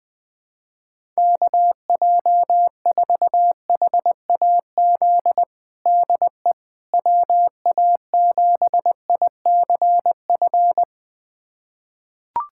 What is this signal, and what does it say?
Morse code, heard as a keyed tone, about 20 wpm, 700 Hz. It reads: KJ4HAZ DE WA7ICF
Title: USA Callsign DE Callsign - 6 Characters - 20wpm